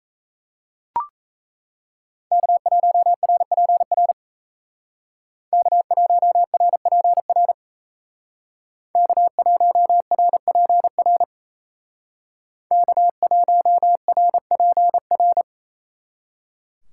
Morse code, heard as a keyed tone, about 35 wpm, 700 Hz. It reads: K1RPR K1RPR K1RPR K1RPR
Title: Speed Racing - USA Call Signs 28wpm